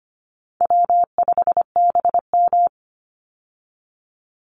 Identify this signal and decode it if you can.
Morse code, heard as a keyed tone, about 25 wpm, 700 Hz. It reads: W5BM
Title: Easy Call Signs Repeated 3x - 25wpm